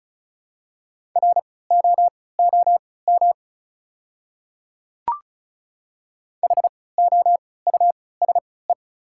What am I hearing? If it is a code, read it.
Morse code, heard as a keyed tone, about 34 wpm, spaced as for 20 wpm, 700 Hz. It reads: ROOM HOUSE
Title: 3x Character Spacing - Top 300 Words - 35wpm